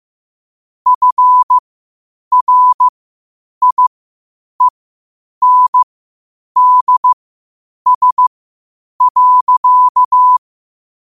Morse code, heard as a keyed tone, about 15 wpm, spaced as for 8 wpm, 1000 Hz. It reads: FRIENDS.
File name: qotdfiles / 8wpm